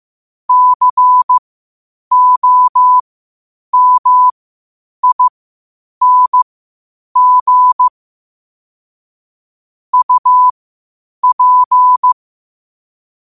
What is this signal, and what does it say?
Morse code, heard as a keyed tone, about 15 wpm, spaced as for 8 wpm, 1000 Hz. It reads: COMING UP
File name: 8wpm